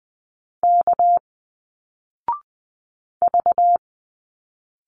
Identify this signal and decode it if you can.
Morse code, heard as a keyed tone, about 20 wpm, 700 Hz. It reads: K V